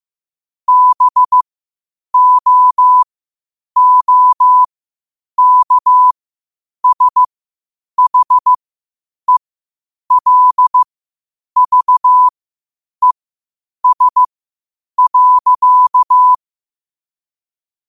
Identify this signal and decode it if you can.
Morse code, heard as a keyed tone, about 15 wpm, spaced as for 8 wpm, 1000 Hz. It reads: BOOKSHELVES.
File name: qotdfiles / 8wpm